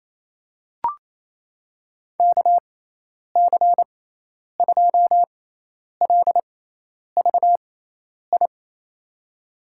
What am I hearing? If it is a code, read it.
Morse code, heard as a keyed tone, about 27 wpm, spaced as for 10 wpm, 700 Hz. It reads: KC2LVI
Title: USA Call Signs - 6 Characters - 6x Character Spacing - 28wpm